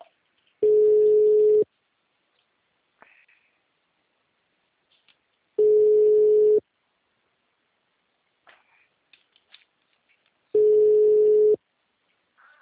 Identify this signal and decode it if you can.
Ringback tone (Europe, 425 Hz).